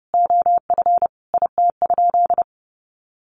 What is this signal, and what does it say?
Morse code, heard as a keyed tone, about 30 wpm, 700 Hz. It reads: OF IT?
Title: Sentences from Top 100 Words 30wpm